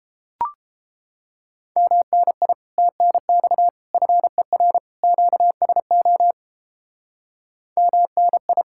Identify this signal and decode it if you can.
Morse code, heard as a keyed tone, about 31 wpm, 700 Hz. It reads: MNI TNX FER QSO MNI